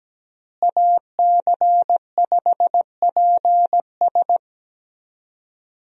Morse code, heard as a keyed tone, about 17 wpm, 700 Hz. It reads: AC5PS
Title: USA Call Signs - 5 Characters - Repeated 3x - 17wpm